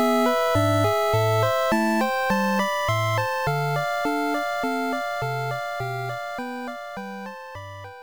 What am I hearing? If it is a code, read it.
There is no code.